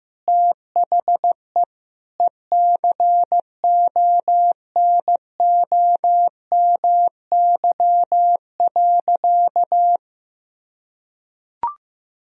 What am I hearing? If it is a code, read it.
Morse code, heard as a keyed tone, about 15 wpm, 700 Hz. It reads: THE ECONOMY.
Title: Sentences from Top 1000 Words - Review Part 3 - 15wpm